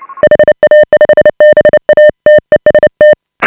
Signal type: Morse code